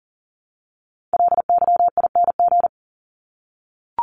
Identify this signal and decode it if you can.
Morse code, heard as a keyed tone, about 40 wpm, 700 Hz. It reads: LYING
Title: ING Suffix - Top 100 Words with Base Words 40wpm